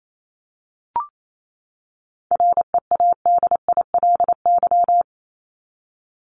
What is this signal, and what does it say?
Morse code, heard as a keyed tone, about 28 wpm, 700 Hz. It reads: READILY
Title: LY Suffix - Top 100 Words with Base Words 28wpm